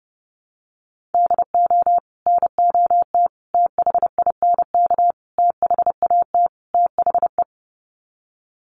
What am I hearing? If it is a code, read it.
Morse code, heard as a keyed tone, about 30 wpm, 700 Hz. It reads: DO NOT THINK THAT THE
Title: Sets of 5 Words Encore2 30wpm